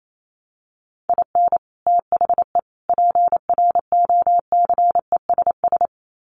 Morse code, heard as a keyed tone, about 28 wpm, 700 Hz. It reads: IN THE PROCESS
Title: Sets of 3 Words 28wpm